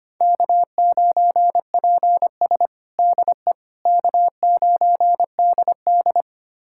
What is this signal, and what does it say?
Morse code, heard as a keyed tone, about 25 wpm, 700 Hz. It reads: K9PS DE K9DD
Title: USA Callsign DE Callsign - 4 Characters - 25wpm